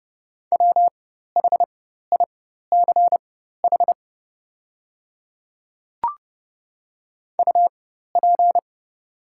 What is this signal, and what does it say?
Morse code, heard as a keyed tone, about 29 wpm, spaced as for 14 wpm, 700 Hz. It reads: WHICH UP
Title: 4x Character Spacing - Top 200 Words - 30wpm